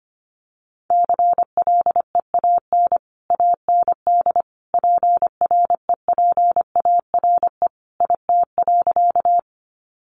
Morse code, heard as a keyed tone, about 25 wpm, 700 Hz. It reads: CLEAN AND PREPARE IT.
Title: Sentences from Top 1100 Words 25wpm